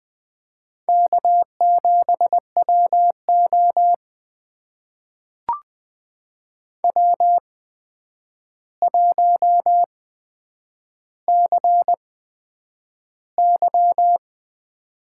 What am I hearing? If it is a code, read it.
Morse code, heard as a keyed tone, about 20 wpm, spaced as for 9 wpm, 700 Hz. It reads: K7WO W1CY
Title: USA Call Signs - 4 Characters - 8x Character Spacing - 20wpm